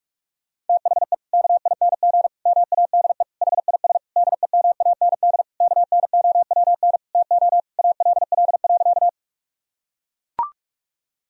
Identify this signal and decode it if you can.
Morse code, heard as a keyed tone, about 41 wpm, 700 Hz. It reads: THE KING MADE HIS DEMAND KNOWN TO ALL.
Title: Sentences from Top 1000 Words - Review Part 4 - 45wpm